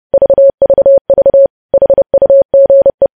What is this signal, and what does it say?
Morse code, heard as a keyed tone, about 30 wpm, 550 Hz. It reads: VVV HUGE
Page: news headlines in morse code at 30 wpm